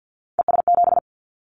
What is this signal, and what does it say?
Morse code, heard as a keyed tone, about 50 wpm, 700 Hz. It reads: ES7
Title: Callsign Prefixes for Top 50 DXCC - 50wpm